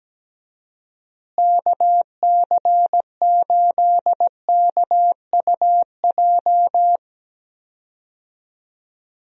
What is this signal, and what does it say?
Morse code, heard as a keyed tone, about 17 wpm, 700 Hz. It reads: KC8KUJ
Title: USA Call Signs - 6 Characters - Repeated 3x - 17wpm